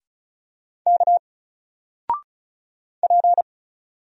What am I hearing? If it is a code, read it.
Morse code, heard as a keyed tone, about 36 wpm, 700 Hz. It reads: K P
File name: Single-Letter-Number-35wpm